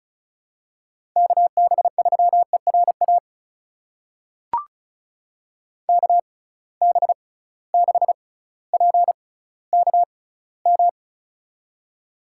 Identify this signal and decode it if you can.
Morse code, heard as a keyed tone, about 36 wpm, spaced as for 18 wpm, 700 Hz. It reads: KB3ERA KB6PKM
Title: USA Call Signs - 6 Characters - 6x Character Spacing - 35wpm